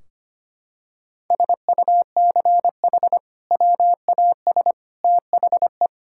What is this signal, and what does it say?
Morse code, heard as a keyed tone, about 25 wpm, 700 Hz. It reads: SUCH WAS THE
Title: Sentences from Top 1600 Words 25wpm